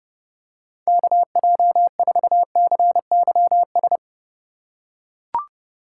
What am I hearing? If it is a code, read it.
Morse code, heard as a keyed tone, about 30 wpm, 700 Hz. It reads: KJ4CYS
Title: USA Call Signs - 3x Character Spacing - 30wpm